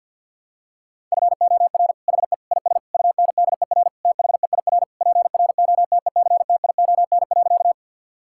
Morse code, heard as a keyed tone, about 50 wpm, 700 Hz. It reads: FOR HE IS UNDER THEIR PROTECTION.
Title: Sentences from Top 1600 Words 50wpm